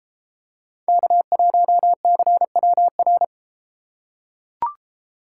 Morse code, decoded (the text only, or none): K1CWR